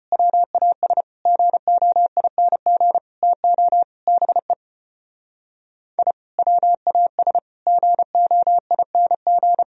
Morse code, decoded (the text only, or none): WAS GOING TO BE I WAS GOING